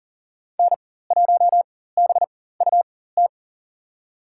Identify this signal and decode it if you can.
Morse code, heard as a keyed tone, about 38 wpm, spaced as for 18 wpm, 700 Hz. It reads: N1BUT
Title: USA Call Signs - 5 Characters - 4x Character Spacing - 40wpm